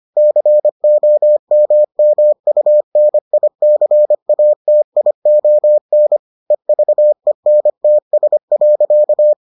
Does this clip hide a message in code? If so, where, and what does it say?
Yes: Morse code from 0.2 to 9.3 s, heard as a keyed tone, about 25 wpm, 600 Hz. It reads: COMMUNICATION EVENTS.